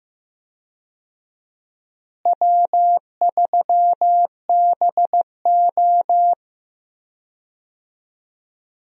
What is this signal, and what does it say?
Morse code, heard as a keyed tone, about 15 wpm, 700 Hz. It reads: W3BO
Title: USA Call Signs - 4 Characters - Repeated 2x - 15wpm